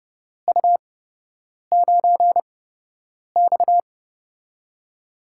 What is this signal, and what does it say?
Morse code, heard as a keyed tone, about 30 wpm, spaced as for 8 wpm, 700 Hz. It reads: U9X
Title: USA Call Signs - 8x Character Spacing - 30wpm